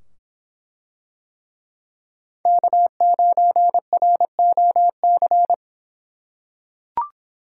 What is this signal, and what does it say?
Morse code, heard as a keyed tone, about 24 wpm, 700 Hz. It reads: K9ROC